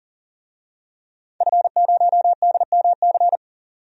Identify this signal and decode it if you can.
Morse code, heard as a keyed tone, about 40 wpm, 700 Hz. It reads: F0DMC
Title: DXCC-100 Call Signs - 5 Characters - 40wpm